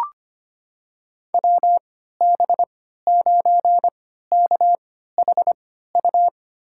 Morse code, heard as a keyed tone, about 25 wpm, spaced as for 14 wpm, 700 Hz. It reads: WB9KHU
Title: USA Call Signs - 6 Characters - 3x Character Spacing - 25wpm